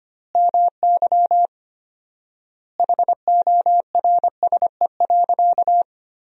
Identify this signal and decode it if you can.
Morse code, heard as a keyed tone, about 25 wpm, 700 Hz. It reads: MY HORSE.
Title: 4x Word-Spacing - Sentences from Top 500 Words - 25wpm